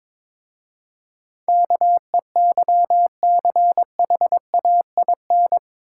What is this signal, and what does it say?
Morse code, heard as a keyed tone, about 23 wpm, 700 Hz. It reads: KEYCHAIN
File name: practice / 22wpm / Difficult Words with Hints - 22wpm